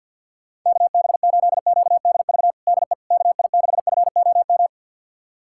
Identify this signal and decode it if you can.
Morse code, heard as a keyed tone, about 47 wpm, 700 Hz. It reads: KB8QDV DE KI6FOM
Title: USA Callsign DE Callsign - 6 Characters - 50wpm